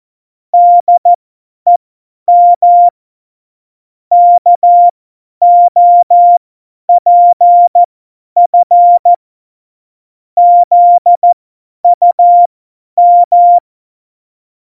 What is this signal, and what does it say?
Morse code, heard as a keyed tone, about 14 wpm, spaced as for 10 wpm, 700 Hz. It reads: DEM KOPF ZUM
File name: Morsekurs / Witze / Witz 27 March 2018, 50 F70